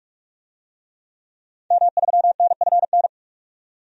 Morse code, heard as a keyed tone, about 47 wpm, 700 Hz. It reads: M3NFN
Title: DXCC-100 Call Signs - 5 Characters - 45wpm